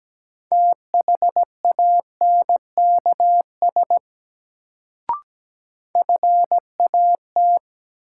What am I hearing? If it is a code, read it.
Morse code, heard as a keyed tone, about 17 wpm, 700 Hz. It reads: THANKS FAT
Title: Sentences from Top 600 Words - New Word Drill - 17wpm